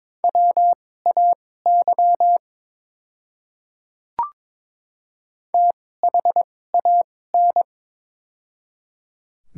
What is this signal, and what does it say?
Morse code, heard as a keyed tone, about 22 wpm, spaced as for 16 wpm, 700 Hz. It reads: WAY THAN
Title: 2x Character Spacing - Top 100 Words - 22wpm